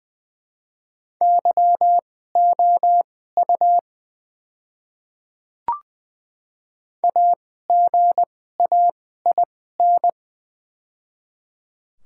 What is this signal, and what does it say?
Morse code, heard as a keyed tone, about 20 wpm, spaced as for 14 wpm, 700 Hz. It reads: YOU AGAIN
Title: 2x Character Spacing - Top 200 Words - 20wpm